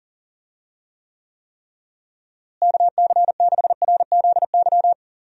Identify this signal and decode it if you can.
Morse code, heard as a keyed tone, about 40 wpm, 700 Hz. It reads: KC6RZY
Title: USA Call Signs - 6 Characters - Repeated 3x - 40wpm